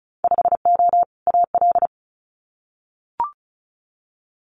Morse code, heard as a keyed tone, about 36 wpm, 700 Hz. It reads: 5O AL
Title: Winter Field Day Exchanges 35wpm